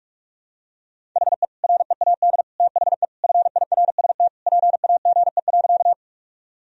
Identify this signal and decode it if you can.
Morse code, heard as a keyed tone, about 46 wpm, 700 Hz. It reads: HE READ THE FIRST PAGE.
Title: Sentences from Top 1000 Words - Review Part 1 - 45wpm